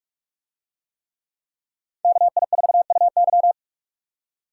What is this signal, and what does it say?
Morse code, heard as a keyed tone, about 46 wpm, 700 Hz. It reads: KI4UY